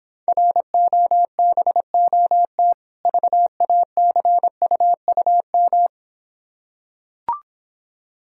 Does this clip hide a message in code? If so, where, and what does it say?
Yes: Morse code from 0.3 to 5.9 s, heard as a keyed tone, about 26 wpm, 700 Hz. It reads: ROBOT VACUUM